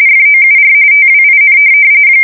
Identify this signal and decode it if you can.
RTTY, about 45.45 baud. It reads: FOUR TWIN
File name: fsk170 / USS-New-Jersey